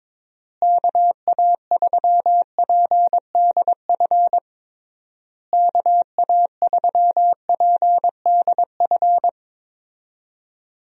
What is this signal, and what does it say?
Morse code, heard as a keyed tone, about 22 wpm, 700 Hz. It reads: KA3PDF KA3PDF
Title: USA Call Signs - 6 Characters - Repeated 3x - 22wpm